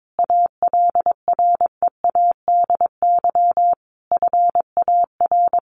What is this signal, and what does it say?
Morse code, heard as a keyed tone, about 22 wpm, 700 Hz. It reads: ALREADY FAR
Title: Sentences from Top 1500 Words 22wpm